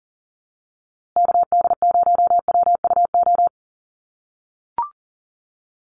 Morse code, heard as a keyed tone, about 40 wpm, 700 Hz. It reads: KD0WUO
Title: USA Call Signs Repeated 3x 40wpm